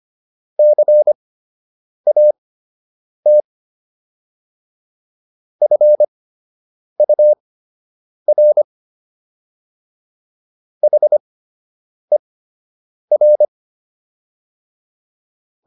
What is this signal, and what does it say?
Morse code, heard as a keyed tone, about 25 wpm, spaced as for 8 wpm, 600 Hz. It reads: CAT FUR HER